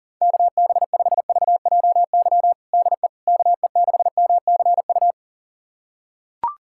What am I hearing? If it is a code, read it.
Morse code, heard as a keyed tone, about 37 wpm, 700 Hz. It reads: KB5VJY DE KE6MCU
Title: USA Callsign DE Callsign 40wpm